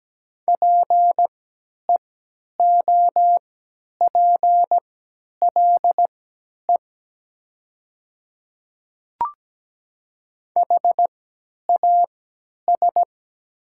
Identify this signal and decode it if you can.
Morse code, heard as a keyed tone, about 17 wpm, spaced as for 10 wpm, 700 Hz. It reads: PEOPLE HAS
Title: 3x Character Spacing - Top 100 Words - 17wpm